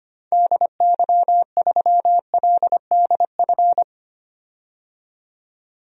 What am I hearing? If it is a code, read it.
Morse code, heard as a keyed tone, about 25 wpm, 700 Hz. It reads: DY3LDF
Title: DXCC-100 Call Signs - 6 Characters - 25wpm